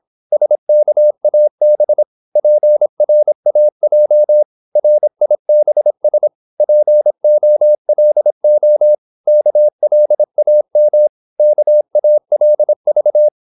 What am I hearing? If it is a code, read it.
Morse code, heard as a keyed tone, about 26 wpm, 600 Hz. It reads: SKAB PRAJ RIBS POLO KLAM KALV
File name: Korte ord DK 26wpm